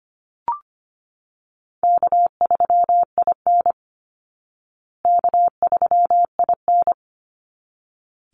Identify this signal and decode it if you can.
Morse code, heard as a keyed tone, about 25 wpm, 700 Hz. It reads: K3IN K3IN
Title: Easy Call Signs Repeated 2x - 25wpm